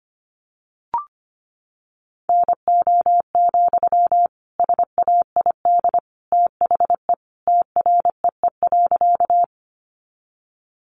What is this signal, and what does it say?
Morse code, heard as a keyed tone, about 25 wpm, 700 Hz. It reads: NO, SAID THE TREE.
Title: Sentences from Top 300 Words 25wpm